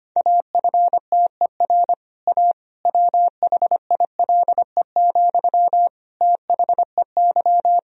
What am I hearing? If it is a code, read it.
Morse code, heard as a keyed tone, about 25 wpm, 700 Hz. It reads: AFTER A WHILE, THEY
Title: Sentences from Top 1000 Words - Review Part 1 - 25wpm